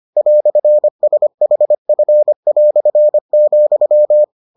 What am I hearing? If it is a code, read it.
Morse code, heard as a keyed tone, about 25 wpm, 600 Hz. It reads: "SHF",